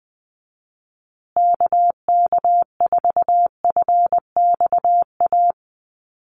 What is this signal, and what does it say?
Morse code, heard as a keyed tone, about 19 wpm, 700 Hz. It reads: KK4FXA
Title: USA Call Signs Repeated 2x 20wpm